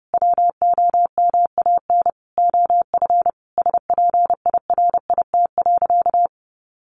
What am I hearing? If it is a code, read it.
Morse code, heard as a keyed tone, about 30 wpm, 700 Hz. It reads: WOMAN OF SPIRIT.
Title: Sentences from Top 1400 Words 30wpm